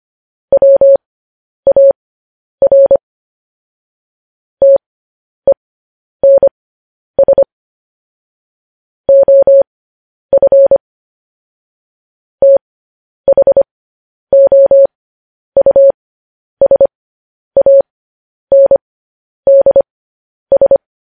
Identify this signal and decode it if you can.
Morse code, heard as a keyed tone, about 25 wpm, spaced as for 10 wpm, 550 Hz. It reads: WAR TENS OF THOUSANDS
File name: podcast / News headlines-1x-25x10